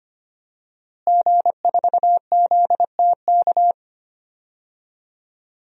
Morse code, heard as a keyed tone, about 25 wpm, 700 Hz. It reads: G4ZTK
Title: DXCC-100 Call Signs - 5 Characters - Repeated 2x - 25wpm